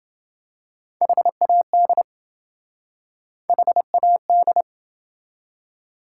Morse code, heard as a keyed tone, about 27 wpm, 700 Hz. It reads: HAD HAD